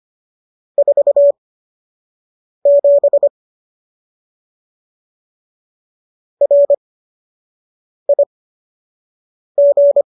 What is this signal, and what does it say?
Morse code, heard as a keyed tone, about 25 wpm, spaced as for 6 wpm, 570 Hz. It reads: 47 RIG